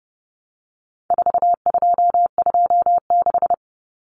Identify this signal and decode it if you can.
Morse code, heard as a keyed tone, about 30 wpm, 700 Hz. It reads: 4226
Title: Common Cut Numbers - 4-Digit 30wpm